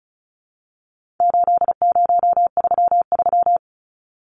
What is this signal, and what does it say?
Morse code, heard as a keyed tone, about 34 wpm, 700 Hz. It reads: 8033
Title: Numbers - 4-Digit 35wpm